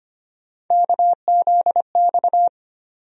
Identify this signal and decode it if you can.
Morse code, heard as a keyed tone, about 25 wpm, 700 Hz. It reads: KZX
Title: USA Call Sign Postfixes - 3 Letters 25wpm